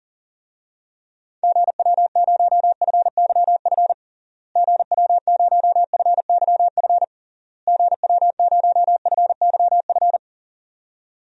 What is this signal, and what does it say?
Morse code, heard as a keyed tone, about 40 wpm, 700 Hz. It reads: GW0FYF GW0FYF GW0FYF
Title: DXCC-100 Call Signs - 6 Characters - Repeated 3x - 40wpm